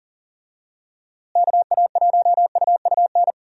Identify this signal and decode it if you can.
Morse code, heard as a keyed tone, about 39 wpm, 700 Hz. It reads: KA1UUN
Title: USA Call Signs - 6 Characters - Repeated 2x - 40wpm